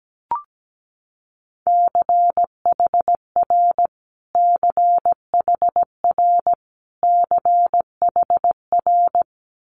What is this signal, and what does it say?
Morse code, heard as a keyed tone, about 17 wpm, 700 Hz. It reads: CHR CHR CHR
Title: Multi-letter Phonemes Repeated 3x 17wpm